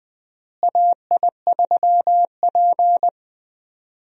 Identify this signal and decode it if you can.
Morse code, heard as a keyed tone, about 20 wpm, 700 Hz. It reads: AI3P
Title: USA Call Signs - 4 Characters - Repeated 3x - 20wpm